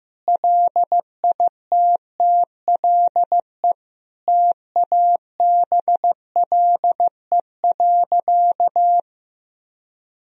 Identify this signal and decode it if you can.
Morse code, heard as a keyed tone, about 15 wpm, 700 Hz. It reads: LITTLE TABLE.
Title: Sentences from Top 1000 Words - Review Part 2 - 15wpm